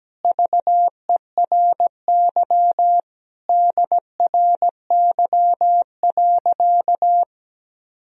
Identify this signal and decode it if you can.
Morse code, heard as a keyed tone, about 17 wpm, 700 Hz. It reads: VERY DRY.
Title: Sentences from Top 1000 Words - Review Part 3 - 17wpm